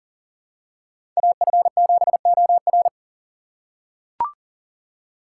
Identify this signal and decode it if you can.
Morse code, heard as a keyed tone, about 37 wpm, 700 Hz. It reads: AF7OR
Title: USA Call Signs - 5 Characters - 6x Character Spacing - 40wpm